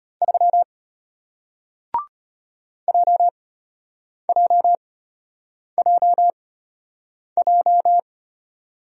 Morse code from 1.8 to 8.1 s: JJJJ

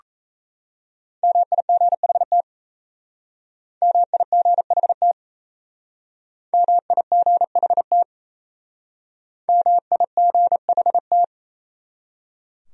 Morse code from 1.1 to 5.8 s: MIGHT MIGHT

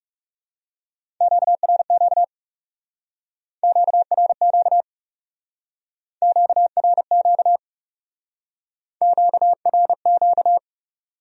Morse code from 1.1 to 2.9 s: QRQ

